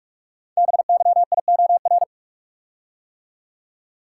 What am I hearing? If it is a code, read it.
Morse code, heard as a keyed tone, about 44 wpm, 700 Hz. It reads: BYIOR
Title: Code Groups - 5 Letters - 45wpm